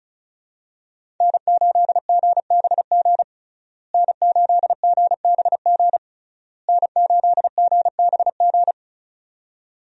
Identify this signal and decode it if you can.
Morse code, heard as a keyed tone, about 36 wpm, 700 Hz. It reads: N8GBG N8GBG N8GBG
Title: USA Call Signs - 5 Characters - Repeated 3x - 35wpm